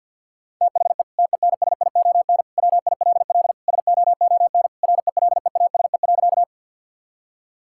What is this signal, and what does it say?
Morse code, heard as a keyed tone, about 50 wpm, 700 Hz. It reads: THE TENSION WILL SOON RELEASE.
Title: Sentences from Top 1600 Words 50wpm